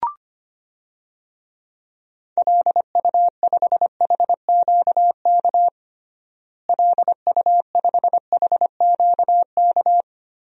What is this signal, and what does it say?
Morse code, heard as a keyed tone, about 25 wpm, 700 Hz. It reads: LU5HQK LU5HQK
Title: DXCC-100 Call Signs - 6 Characters - Repeated 2x - 25wpm